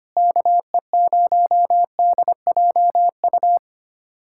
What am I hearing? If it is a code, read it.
Morse code, heard as a keyed tone, about 25 wpm, 700 Hz. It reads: KE0DJU
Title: USA Callsign DE Callsign 25wpm